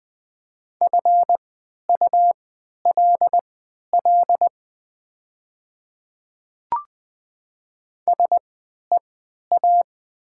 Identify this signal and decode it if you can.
Morse code, heard as a keyed tone, about 20 wpm, spaced as for 11 wpm, 700 Hz. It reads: FULL SEA
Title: 3x Character Spacing - Top 500 Words - 20wpm